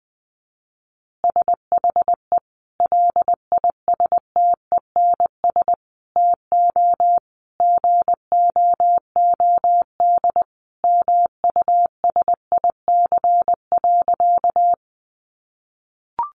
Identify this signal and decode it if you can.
Morse code, heard as a keyed tone, about 19 wpm, 700 Hz. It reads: SHE LISTENS TO GOOD MUSIC.
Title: Easy Phrases 20wpm